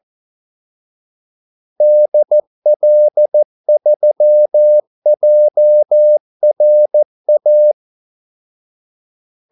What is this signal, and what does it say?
Morse code, heard as a keyed tone, about 14 wpm, 600 Hz. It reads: DL3JRA
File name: Callsigns-14wpm